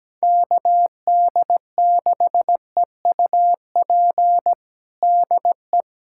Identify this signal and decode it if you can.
Morse code, heard as a keyed tone, about 17 wpm, 700 Hz. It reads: KD6EUP DE
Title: USA Callsign DE Callsign - 6 Characters - 17wpm